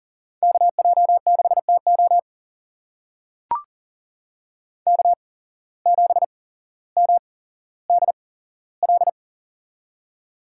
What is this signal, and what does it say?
Morse code, heard as a keyed tone, about 41 wpm, spaced as for 17 wpm, 700 Hz. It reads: KJ6TO K7MDL
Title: USA Call Signs - 5 Characters - 8x Character Spacing - 40wpm